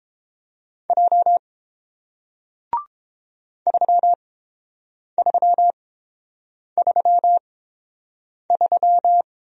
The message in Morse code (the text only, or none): J 3333